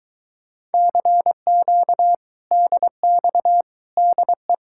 Morse code, heard as a keyed tone, about 22 wpm, 700 Hz. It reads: CQ DX DE